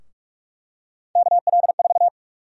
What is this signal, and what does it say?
Morse code, heard as a keyed tone, about 45 wpm, 700 Hz. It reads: KL4
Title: USA Call Sign Prefixes - Repeated 2x - 45wpm